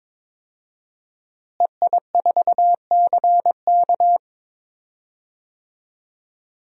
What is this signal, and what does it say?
Morse code, heard as a keyed tone, about 22 wpm, 700 Hz. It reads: EI4CK
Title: DXCC-100 Call Signs - 5 Characters - Repeated 3x - 22wpm